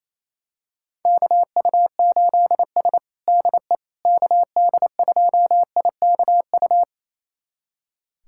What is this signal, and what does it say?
Morse code, heard as a keyed tone, about 28 wpm, 700 Hz. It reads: KU8S DE KD2IKU
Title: USA Callsign DE Callsign 28wpm